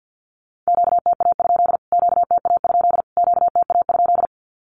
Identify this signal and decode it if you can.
Morse code, heard as a keyed tone, about 50 wpm, 700 Hz. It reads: QTA? QTA? QTA?
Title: Q-Codes Repeated 3x 50wpm